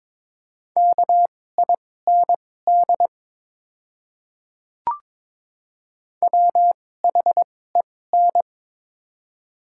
Morse code, heard as a keyed tone, about 23 wpm, spaced as for 16 wpm, 700 Hz. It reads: KIND WHEN
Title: 2x Character Spacing - Top 200 Words - 22wpm